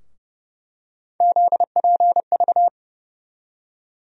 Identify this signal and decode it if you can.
Morse code, heard as a keyed tone, about 30 wpm, 700 Hz. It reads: ZPV
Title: USA Call Sign Postfixes - 3 Letters 30wpm